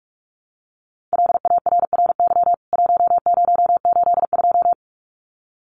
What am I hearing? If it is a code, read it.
Morse code, heard as a keyed tone, about 45 wpm, 700 Hz. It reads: LARRY 1082
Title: CWT Exchange 45wpm